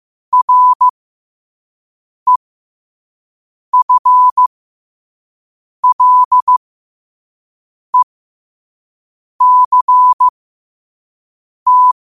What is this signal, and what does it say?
Morse code, heard as a keyed tone, about 15 wpm, spaced as for 5 wpm, 1000 Hz. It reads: REFLECT
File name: qotdfiles / 5wpm